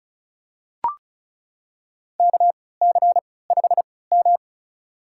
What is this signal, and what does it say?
Morse code, heard as a keyed tone, about 34 wpm, spaced as for 20 wpm, 700 Hz. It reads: KC5M
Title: USA Call Signs - 4 Characters - 3x Character Spacing - 35wpm